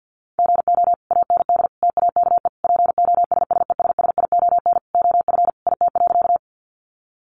Morse code, heard as a keyed tone, about 50 wpm, 700 Hz. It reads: GO AND TAKE POSSESSION OF IT.
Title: Sentences from Top 1400 Words 50wpm